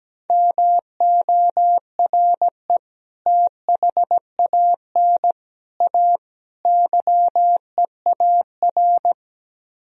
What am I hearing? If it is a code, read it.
Morse code, heard as a keyed tone, about 17 wpm, 700 Hz. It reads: MORE THAN A YEAR
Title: Speed-Racing - Sets of 4 Words 17wpm